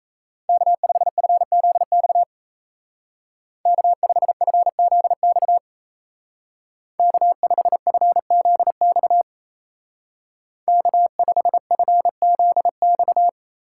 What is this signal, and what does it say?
Morse code, heard as a keyed tone, about 35 wpm, 700 Hz. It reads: K5FZX K5FZX K5FZX K5FZX